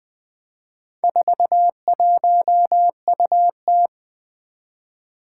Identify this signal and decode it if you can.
Morse code, heard as a keyed tone, about 20 wpm, 700 Hz. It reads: 41UT
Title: Four Challenging Letter-Number Combinations 20wpm